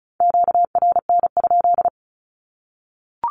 Morse code, heard as a keyed tone, about 34 wpm, 700 Hz. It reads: QRN?
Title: Q-Codes Repeated 3x 35wpm